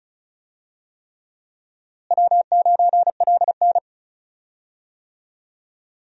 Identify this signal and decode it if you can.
Morse code, heard as a keyed tone, about 36 wpm, 700 Hz. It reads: W9LN
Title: USA Call Signs - 4 Characters - Repeated 3x - 35wpm